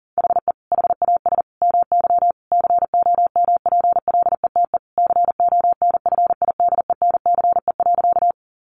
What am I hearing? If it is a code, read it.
Morse code, heard as a keyed tone, about 40 wpm, 700 Hz. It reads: HE HAS MY COMPLETE CONFIDENCE.
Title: Sentences from Top 1500 Words 40wpm